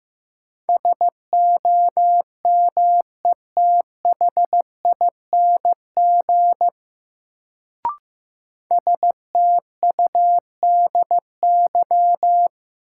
Morse code, heard as a keyed tone, about 15 wpm, 700 Hz. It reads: SOMETHING STUDY
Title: Sets of 5 Words - New Word Drill - 15wpm